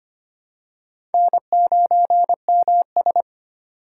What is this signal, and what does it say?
Morse code, heard as a keyed tone, about 25 wpm, 700 Hz. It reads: N9MS
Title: K1USN SST QSO Practice - 25wpm